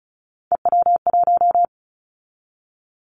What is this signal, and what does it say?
Morse code, heard as a keyed tone, about 36 wpm, 700 Hz. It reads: EW1
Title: Callsign Prefixes for Top 50 DXCC - 35wpm